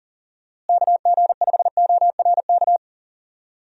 Morse code, heard as a keyed tone, about 39 wpm, 700 Hz. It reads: KG5ORK
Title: USA Call Signs - 6 Characters - Repeated 2x - 40wpm